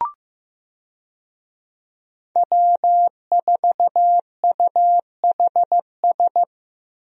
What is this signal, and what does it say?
Morse code, heard as a keyed tone, about 15 wpm, 700 Hz. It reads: W4UHS
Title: USA Call Signs - 5 Characters - Repeated 2x - 15wpm